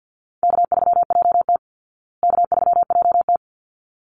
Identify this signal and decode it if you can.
Morse code, heard as a keyed tone, about 48 wpm, 700 Hz. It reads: K3JT K3JT
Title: Easy Call Signs Repeated 2x - 50wpm